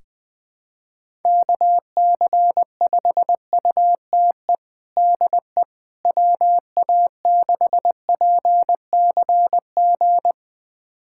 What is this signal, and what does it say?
Morse code, heard as a keyed tone, about 20 wpm, 700 Hz. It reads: KC5UTE DE WA6PCG
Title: USA Callsign DE Callsign 20wpm